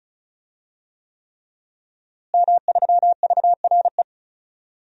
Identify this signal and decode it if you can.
Morse code, heard as a keyed tone, about 34 wpm, 700 Hz. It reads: M3VRE
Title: DXCC-100 Call Signs - 5 Characters - Repeated 3x - 35wpm